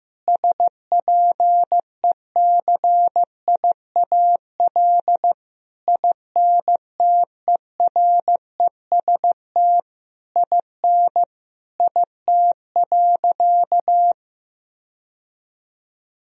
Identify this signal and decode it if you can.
Morse code, heard as a keyed tone, about 15 wpm, 700 Hz. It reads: SPECIAL INTEREST IN IT.
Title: Sentences from Top 1000 Words - Review Part 3 - 15wpm